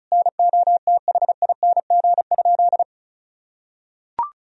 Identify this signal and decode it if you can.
Morse code, heard as a keyed tone, about 36 wpm, 700 Hz. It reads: NOTHING?
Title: Sentences from Top 1000 Words - Review Part 4 - 35wpm